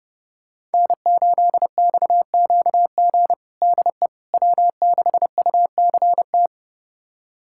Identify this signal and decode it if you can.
Morse code, heard as a keyed tone, about 32 wpm, 700 Hz. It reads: N8XQG DE W6UCT
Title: USA Callsign DE Callsign - 5 Characters - 30wpm